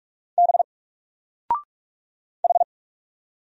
Morse code, heard as a keyed tone, about 46 wpm, 700 Hz. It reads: B H
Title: Single Letter-Number Alt 45wpm